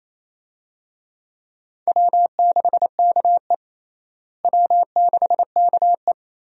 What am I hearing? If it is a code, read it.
Morse code, heard as a keyed tone, about 28 wpm, 700 Hz. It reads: W6KE W6KE
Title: USA Call Signs - 4 Characters - Repeated 3x - 28wpm